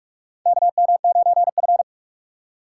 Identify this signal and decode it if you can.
Morse code, heard as a keyed tone, about 46 wpm, 700 Hz. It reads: KM9F
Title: USA Call Signs - 4 Characters - 3x Character Spacing - 45wpm